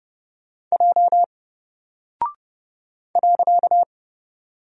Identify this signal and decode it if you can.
Morse code, heard as a keyed tone, about 31 wpm, 700 Hz. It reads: J .
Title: Single Character (Letter:Number:Punctuation:Prosign) 30wpm